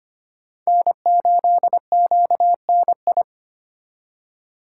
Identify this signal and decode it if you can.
Morse code, heard as a keyed tone, about 32 wpm, 700 Hz. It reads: N8QNI